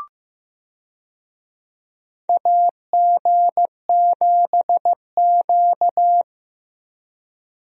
Morse code, heard as a keyed tone, about 15 wpm, 700 Hz. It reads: AG7Q
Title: USA Call Signs - 4 Characters - Repeated 2x - 15wpm